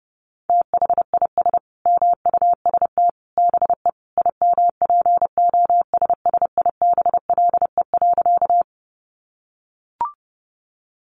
Morse code, heard as a keyed tone, about 30 wpm, 700 Hz. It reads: THIS MUST BE IMPOSSIBLE.
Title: Sentences from Top 1100 Words 30wpm